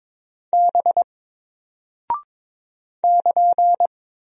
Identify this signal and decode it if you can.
Morse code, heard as a keyed tone, about 22 wpm, 700 Hz. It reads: B <KN>
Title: Single Character (Letter:Number:Punctuation:Prosign) Alt 22wpm